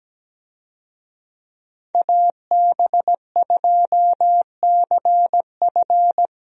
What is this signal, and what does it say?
Morse code, heard as a keyed tone, about 17 wpm, 700 Hz. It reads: AB2CF
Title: USA Call Signs - 5 Characters - 17wpm